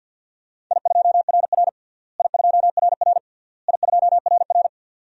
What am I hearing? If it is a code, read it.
Morse code, heard as a keyed tone, about 50 wpm, 700 Hz. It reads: I2RR I2RR I2RR
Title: DXCC-100 Call Signs - 4 Characters - Repeated 3x - 50wpm